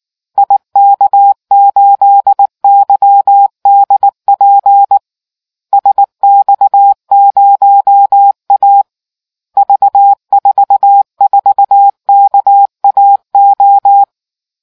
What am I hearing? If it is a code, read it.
Morse code, heard as a keyed tone, about 19 wpm, 800 Hz. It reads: IK8YDP SX0A V44KAO